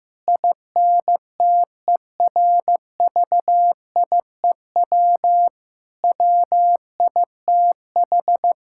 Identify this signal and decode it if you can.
Morse code, heard as a keyed tone, about 16 wpm, 700 Hz. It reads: INTERVIEW WITH
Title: Sentences from Top 1600 Words 15wpm